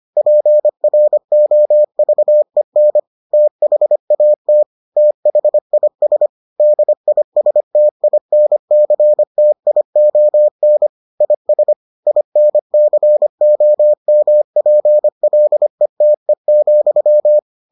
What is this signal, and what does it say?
Morse code, heard as a keyed tone, about 25 wpm, 600 Hz. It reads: PROVEN THAT THIS DISTINCTION IS INCOMPLETE,